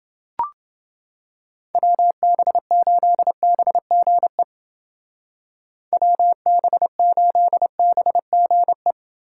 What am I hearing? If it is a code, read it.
Morse code, heard as a keyed tone, about 29 wpm, 700 Hz. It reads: WB8BGE WB8BGE